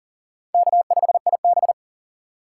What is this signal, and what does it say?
Morse code, heard as a keyed tone, about 37 wpm, 700 Hz. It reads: K5IB
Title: USA Call Signs - 4 Characters - 2x Character Spacing - 40wpm